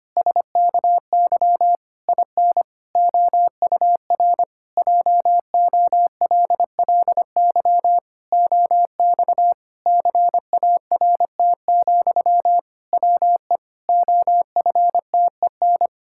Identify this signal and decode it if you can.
Morse code, heard as a keyed tone, about 25 wpm, 700 Hz. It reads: SKY IN OUR JOLLY OX CART, WE OFTEN
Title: Pangram Challenge v2 - 25wpm